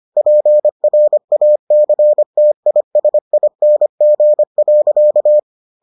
Morse code, heard as a keyed tone, about 25 wpm, 600 Hz. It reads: PRACTISING.